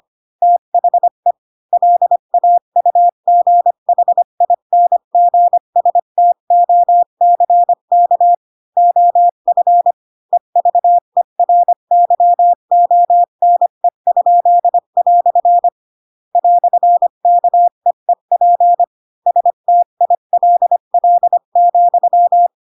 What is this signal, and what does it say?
Morse code, heard as a keyed tone, about 25 wpm, 700 Hz. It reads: THE LAUGHINGSTOCK OF EVERYONE?" "KEEP STILL,
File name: Pinocchio / Pinocchio ch4 20 25 ik1hge